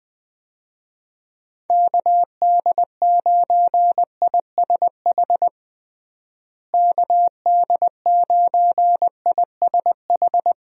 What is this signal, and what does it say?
Morse code, heard as a keyed tone, about 20 wpm, 700 Hz. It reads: KD9ISH KD9ISH
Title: USA Call Signs - 6 Characters - Repeated 3x - 20wpm